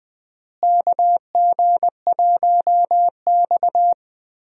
Morse code, heard as a keyed tone, about 20 wpm, 700 Hz. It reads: KG1X
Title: USA Call Signs - 4 Characters - Repeated 2x - 20wpm